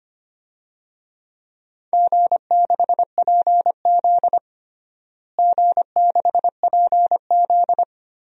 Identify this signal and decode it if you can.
Morse code, heard as a keyed tone, about 25 wpm, 700 Hz. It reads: G6PZ G6PZ
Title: DXCC-100 Call Signs - 4 Characters - Repeated 3x - 25wpm